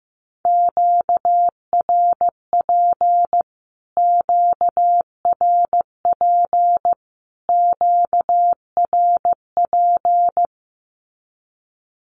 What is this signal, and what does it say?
Morse code, heard as a keyed tone, about 15 wpm, 700 Hz. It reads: QRP QRP QRP
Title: Q-Codes Repeated 3x 15wpm